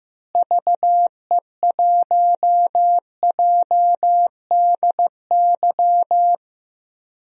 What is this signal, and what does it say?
Morse code, heard as a keyed tone, about 15 wpm, 700 Hz. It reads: VE1JDY
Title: DXCC-100 Call Signs - 6 Characters - Repeated 2x - 15wpm